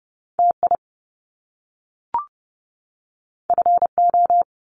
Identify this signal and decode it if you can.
Morse code, heard as a keyed tone, about 30 wpm, 700 Hz. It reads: TI FO
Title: Top Two Letter Combinations in Words 30wpm